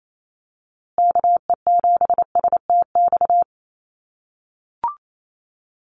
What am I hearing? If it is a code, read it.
Morse code, heard as a keyed tone, about 28 wpm, 700 Hz. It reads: KE7STX
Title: USA Call Signs Repeated 2x 28wpm